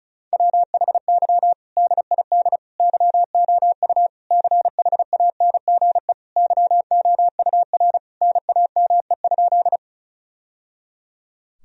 Morse code, heard as a keyed tone, about 34 wpm, 700 Hz. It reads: WHY DID YOU CHANGE YOUR NAME?
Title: Sentences from Top 1000 Words - Review Part 1 - 35wpm